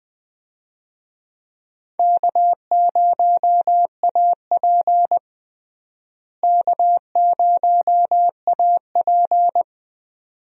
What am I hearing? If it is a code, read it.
Morse code, heard as a keyed tone, about 20 wpm, 700 Hz. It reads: K0AP K0AP
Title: USA Call Signs - 4 Characters - Repeated 2x - 20wpm